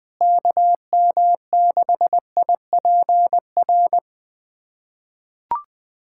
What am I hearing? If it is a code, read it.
Morse code, heard as a keyed tone, about 21 wpm, 700 Hz. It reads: KM6IPR
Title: USA Callsign DE Callsign - 6 Characters - 20wpm